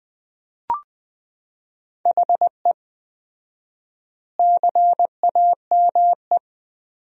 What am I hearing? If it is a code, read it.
Morse code, heard as a keyed tone, about 20 wpm, 700 Hz. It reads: HE CAME
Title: 4x Word-Spacing - Sentences from Top 300 Words - 20wpm